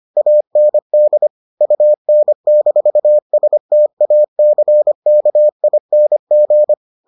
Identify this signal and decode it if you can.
Morse code, heard as a keyed tone, about 25 wpm, 600 Hz. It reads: AND UN-STACKING